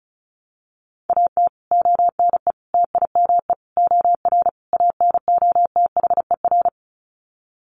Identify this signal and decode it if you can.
Morse code, heard as a keyed tone, about 34 wpm, 700 Hz. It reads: AT ONE TIME OR ANOTHER
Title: Sets of 5 Words Encore 35wpm